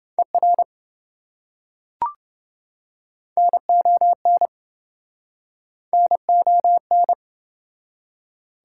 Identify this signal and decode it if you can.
Morse code, heard as a keyed tone, about 26 wpm, 700 Hz. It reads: ER NON NON